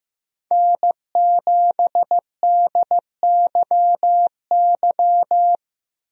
Morse code, heard as a keyed tone, about 16 wpm, 700 Hz. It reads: N7DYY
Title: USA Call Signs - 5 Characters - 15wpm